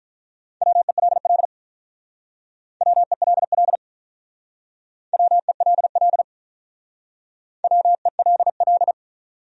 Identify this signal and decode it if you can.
Morse code, heard as a keyed tone, about 45 wpm, 700 Hz. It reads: WELL WELL WELL WELL